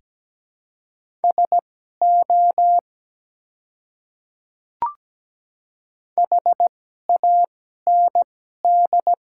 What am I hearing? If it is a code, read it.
Morse code, heard as a keyed tone, about 17 wpm, spaced as for 12 wpm, 700 Hz. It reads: SO HAND
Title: 2x Character Spacing - Top 200 Words - 17wpm